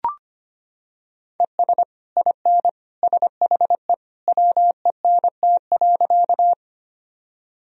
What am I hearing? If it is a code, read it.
Morse code, heard as a keyed tone, about 25 wpm, 700 Hz. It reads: ES IN SHE WENT.
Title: Sentences from Top 200 Words and Abbreviations 25wpm